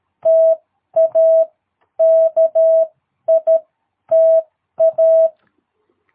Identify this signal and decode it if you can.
Morse code, heard as a keyed tone, about 13 wpm, spaced as for 10 wpm, 660 Hz. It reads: TA KITA
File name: jh8cbh / 2022cw